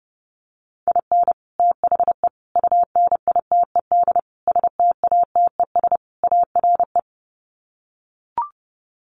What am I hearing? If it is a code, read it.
Morse code, heard as a keyed tone, about 30 wpm, 700 Hz. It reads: IN THE UNITED STATES ARE
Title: Sets of 5 Words Encore 30wpm